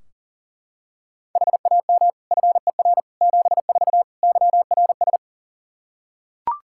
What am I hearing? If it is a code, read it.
Morse code, heard as a keyed tone, about 40 wpm, 700 Hz. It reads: HAM FER 74 YRS
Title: Practice QSO - Exchange 3 - More Info - 40wpm